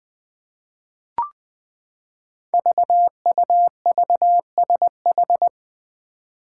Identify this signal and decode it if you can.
Morse code, heard as a keyed tone, about 20 wpm, 700 Hz. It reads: VUVSH
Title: Five Challenging Letter-Number Combinations 20wpm